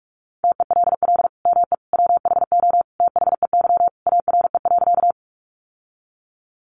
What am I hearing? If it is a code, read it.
Morse code, heard as a keyed tone, about 44 wpm, 700 Hz. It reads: TELL ME WHO THEY ARE.
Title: Sentences from Top 700 Words 45wpm